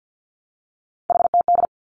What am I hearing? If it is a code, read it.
Morse code, heard as a keyed tone, about 50 wpm, 700 Hz. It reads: HTD
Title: Three Challenging Letter-Number Combinations 50wpm